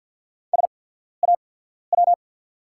Morse code, heard as a keyed tone, about 48 wpm, spaced as for 14 wpm, 700 Hz. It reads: SAW
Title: 8x Character Spacing - Top 500 Words - 50wpm